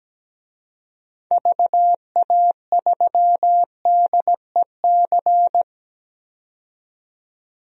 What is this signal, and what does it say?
Morse code, heard as a keyed tone, about 17 wpm, 700 Hz. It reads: VA3DEC
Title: DXCC-100 Call Signs - 6 Characters - 17wpm